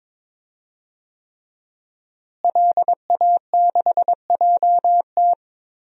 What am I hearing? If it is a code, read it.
Morse code, heard as a keyed tone, about 21 wpm, 700 Hz. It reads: LA6JT
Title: DXCC-100 Call Signs - 5 Characters - Repeated 3x - 22wpm